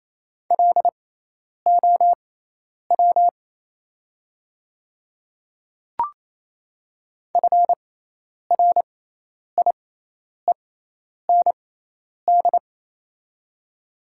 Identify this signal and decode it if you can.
Morse code, heard as a keyed tone, about 28 wpm, spaced as for 10 wpm, 700 Hz. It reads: LOW FRIEND
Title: 6x Character Spacing - Top 500 Words - 28wpm